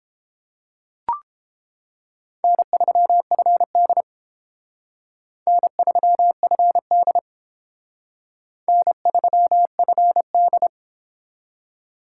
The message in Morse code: N3FD N3FD N3FD

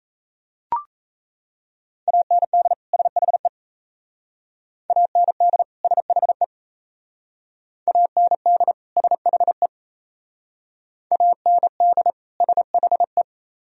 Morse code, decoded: AND SHE AND SHE AND SHE AND SHE

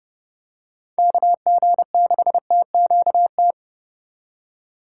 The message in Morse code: KG6TQT

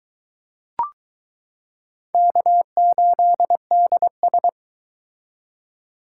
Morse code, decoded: K8DS

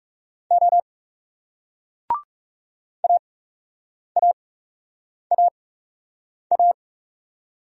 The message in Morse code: O AAAA

